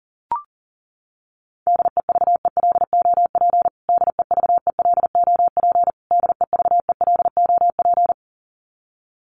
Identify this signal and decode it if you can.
Morse code, heard as a keyed tone, about 40 wpm, 700 Hz. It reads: DEVELOP DEVELOP DEVELOP
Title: Top 500 Words Repeated 40wpm